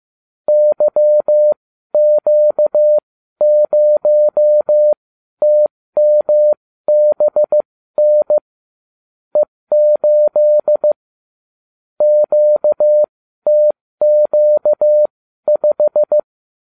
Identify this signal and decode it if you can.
Morse code, heard as a keyed tone, about 15 wpm, spaced as for 12 wpm, 600 Hz. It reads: YQ0TMBN E8 QTQ5